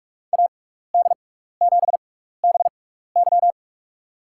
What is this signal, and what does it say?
Morse code, heard as a keyed tone, about 43 wpm, spaced as for 15 wpm, 700 Hz. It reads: AD7BY
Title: USA Call Signs - 5 Characters - 6x Character Spacing - 45wpm